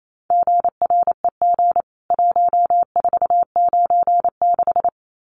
Morse code, heard as a keyed tone, about 28 wpm, 700 Hz. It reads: GREG 1496
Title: CWT Exchange 28wpm